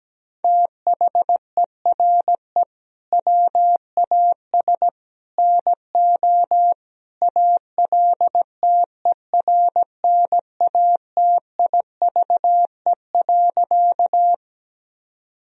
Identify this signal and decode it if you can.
Morse code, heard as a keyed tone, about 17 wpm, 700 Hz. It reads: THERE WAS NO ALTERNATIVE.
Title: Sentences from Top 1000 Words - Review Part 4 - 17wpm